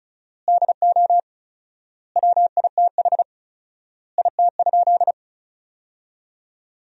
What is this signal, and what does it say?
Morse code, heard as a keyed tone, about 36 wpm, 700 Hz. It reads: DO WITH IT?
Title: 4x Word-Spacing - Sentences from Top 100 Words - 35wpm